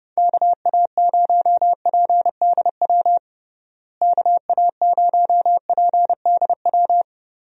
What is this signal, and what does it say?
Morse code, heard as a keyed tone, about 30 wpm, 700 Hz. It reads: KA0PDW KA0PDW
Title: USA Call Signs - 6 Characters - Repeated 3x - 30wpm